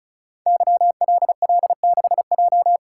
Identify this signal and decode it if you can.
Morse code, heard as a keyed tone, about 36 wpm, 690 Hz. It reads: YLL6J
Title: Code Groups - 5 Letter:Numbers - 35wpm